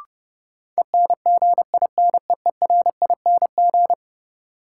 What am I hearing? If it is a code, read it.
Morse code, heard as a keyed tone, about 31 wpm, 700 Hz. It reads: ENGINEERING
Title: Sentences from Top 1500 Words - New Word Drill Alt - 30wpm